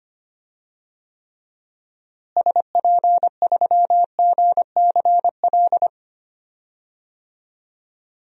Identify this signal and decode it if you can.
Morse code, heard as a keyed tone, about 25 wpm, 700 Hz. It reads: SP3GCL
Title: DXCC-100 Call Signs - 6 Characters - Repeated 2x - 25wpm